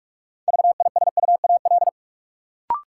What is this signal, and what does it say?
Morse code, heard as a keyed tone, about 46 wpm, 700 Hz. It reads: VISUAL
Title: Sets of 5 Words Encore - New Word Drill - 45wpm